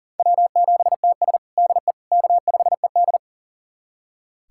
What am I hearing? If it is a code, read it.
Morse code, heard as a keyed tone, about 40 wpm, 700 Hz. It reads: W7TS DE K5ED
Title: USA Callsign DE Callsign - 4 Characters - 40wpm